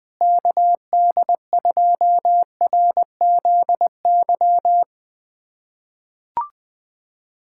Morse code, heard as a keyed tone, about 21 wpm, 700 Hz. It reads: KD2RZY
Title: USA Callsign DE Callsign - 6 Characters - 20wpm